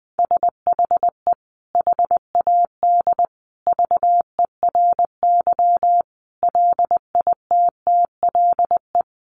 Morse code, heard as a keyed tone, about 20 wpm, 700 Hz. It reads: SHE HAD VERY LITTLE
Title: Sentences from Top 400 Words 20wpm